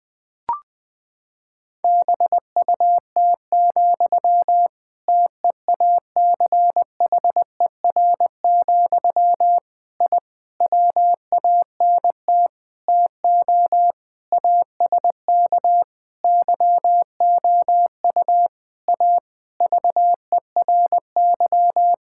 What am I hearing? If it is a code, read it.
Morse code, heard as a keyed tone, about 21 wpm, 700 Hz. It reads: BUT, TEACHER, I WANT TO ASK YOU A VERY
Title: Sentences from Top 1000 Words - Review Part 6 - 20wpm